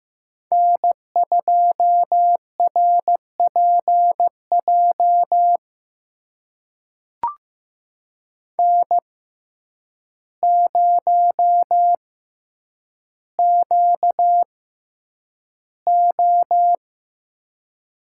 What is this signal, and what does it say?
Morse code, heard as a keyed tone, about 15 wpm, spaced as for 8 wpm, 700 Hz. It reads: N2RPJ N0QO
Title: USA Call Signs - 5 Characters - 6x Character Spacing - 15wpm